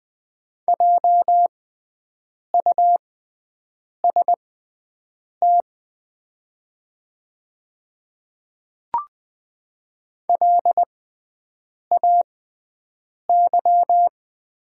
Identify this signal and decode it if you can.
Morse code, heard as a keyed tone, about 20 wpm, spaced as for 7 wpm, 700 Hz. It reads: JUST LAY
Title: 6x Character Spacing - Top 500 Words - 20wpm